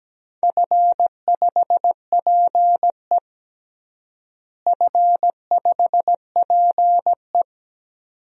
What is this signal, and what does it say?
Morse code, heard as a keyed tone, about 17 wpm, 700 Hz. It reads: F5PE F5PE
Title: DXCC-100 Call Signs - 4 Characters - Repeated 2x - 17wpm